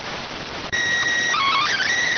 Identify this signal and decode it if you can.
SSTV (slow-scan television), PD 290.